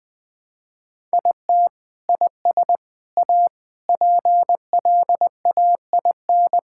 Morse code, heard as a keyed tone, about 20 wpm, 700 Hz. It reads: IT IS A PLAIN
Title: Sentences from Top 1000 Words - Review Part 5 - 20wpm